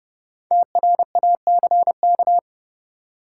Morse code, heard as a keyed tone, about 30 wpm, 700 Hz. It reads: TRACK
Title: Sentences from Top 1100 Words - New Word Drill - 30wpm